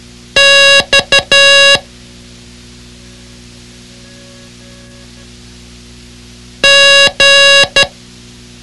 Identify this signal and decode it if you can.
Morse code, heard as a keyed tone, about 12 wpm, 2800 Hz. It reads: X G
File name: ham radio / CodeQuick / Code Quick 06